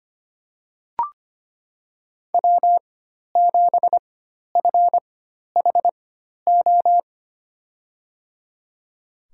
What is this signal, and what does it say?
Morse code, heard as a keyed tone, about 25 wpm, spaced as for 12 wpm, 700 Hz. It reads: W7FHO
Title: USA Call Signs - 4x Character Spacing - 25wpm